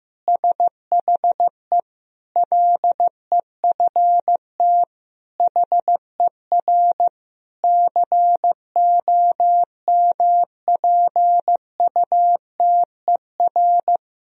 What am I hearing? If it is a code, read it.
Morse code, heard as a keyed tone, about 16 wpm, 700 Hz. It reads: SHE LEFT HER COMPUTER
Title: Sentences from Top 1000 Words - Review Part 3 - 15wpm